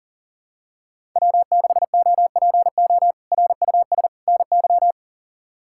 Morse code, heard as a keyed tone, about 40 wpm, 700 Hz. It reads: W6OPO RUS NY
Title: K1USN SST QSO Practice - 40wpm